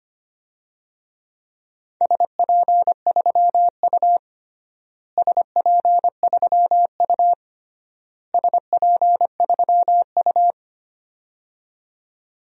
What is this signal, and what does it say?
Morse code, heard as a keyed tone, about 25 wpm, 700 Hz. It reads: SP3U SP3U SP3U
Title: DXCC-100 Call Signs - 4 Characters - Repeated 3x - 25wpm